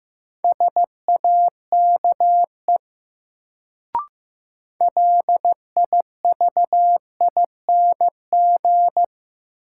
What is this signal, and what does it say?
Morse code, heard as a keyed tone, about 15 wpm, 700 Hz. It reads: SAKE LIVING
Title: Sets of 4 Words - New Word Drill - 15wpm